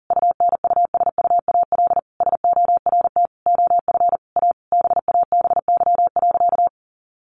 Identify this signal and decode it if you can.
Morse code, heard as a keyed tone, about 40 wpm, 700 Hz. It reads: UNUSUAL SORT OF A BABY.
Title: Sentences from Top 1400 Words 40wpm